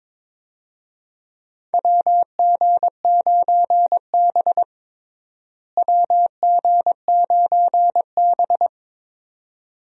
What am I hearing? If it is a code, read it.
Morse code, heard as a keyed tone, about 22 wpm, 700 Hz. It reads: WG9B WG9B
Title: USA Call Signs - 4 Characters - Repeated 2x - 22wpm